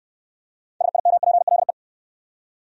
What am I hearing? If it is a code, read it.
Morse code, heard as a keyed tone, about 63 wpm, 700 Hz. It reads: SIMPLE